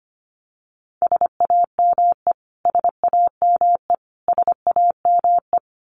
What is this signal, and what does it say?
Morse code, heard as a keyed tone, about 25 wpm, 700 Hz. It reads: SAME SAME SAME
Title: Top 500 Words Repeated 25wpm